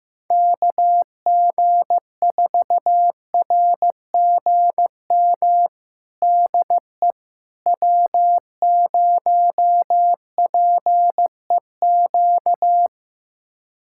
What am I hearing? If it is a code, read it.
Morse code, heard as a keyed tone, about 15 wpm, 700 Hz. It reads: KG4RGM DE W0PEQ
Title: USA Callsign DE Callsign 15wpm